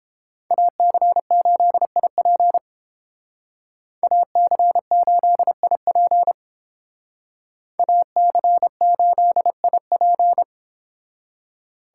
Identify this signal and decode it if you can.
Morse code, heard as a keyed tone, about 30 wpm, 700 Hz. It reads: AC8IP AC8IP AC8IP